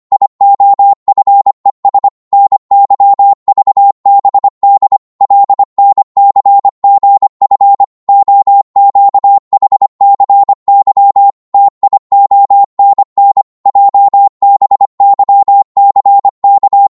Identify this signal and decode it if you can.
Morse code, heard as a keyed tone, about 25 wpm, 800 Hz. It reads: IOFES NYVBD LNCGF OQHCY TIONN JBYCK
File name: Morse / ltr-25wpm